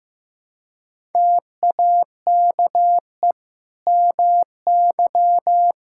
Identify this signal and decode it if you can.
Morse code, heard as a keyed tone, about 15 wpm, 700 Hz. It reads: TAKE MY
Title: Sentences from Top 1000 Words - Review Part 5 - 15wpm